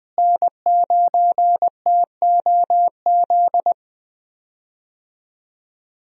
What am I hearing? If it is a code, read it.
Morse code, heard as a keyed tone, about 20 wpm, 700 Hz. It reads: N9TOZ
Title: USA Call Signs - 5 Characters - Repeated 3x - 20wpm